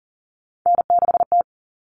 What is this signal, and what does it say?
Morse code, heard as a keyed tone, about 40 wpm, 700 Hz. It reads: N6T
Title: All Cut Numbers - 3-Digit 40wpm